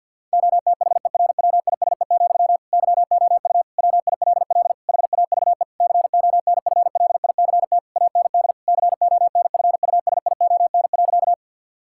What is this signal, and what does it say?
Morse code, heard as a keyed tone, about 50 wpm, 700 Hz. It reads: OTHERWISE, YOU WILL HAVE CONFLICT AND CONFUSION.
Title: Sentences from Top 1600 Words 50wpm